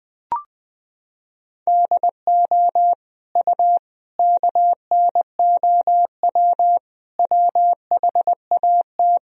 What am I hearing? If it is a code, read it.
Morse code, heard as a keyed tone, about 20 wpm, 700 Hz. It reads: DO U KNOW WHAT
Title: Sentences from Top 100 Words and Abbreviations 20wpm